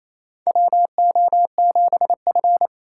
Morse code, heard as a keyed tone, about 28 wpm, 700 Hz. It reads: WO7F
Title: USA Call Signs - 4 Characters - 2x Character Spacing - 28wpm